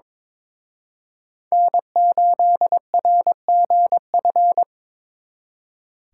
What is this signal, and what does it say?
Morse code, heard as a keyed tone, about 25 wpm, 700 Hz. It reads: N8RGF
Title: Speed Racing - USA 1x3 Call Signs 22wpm